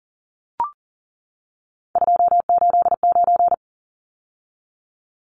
Morse code, heard as a keyed tone, about 40 wpm, 700 Hz. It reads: 289
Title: Numbers - 3-Digit 40wpm